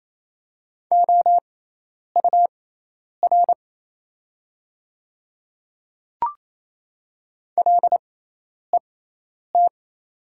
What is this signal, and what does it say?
Morse code, heard as a keyed tone, about 28 wpm, spaced as for 10 wpm, 700 Hz. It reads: OUR LET
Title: 6x Character Spacing - Top 300 Words - 28wpm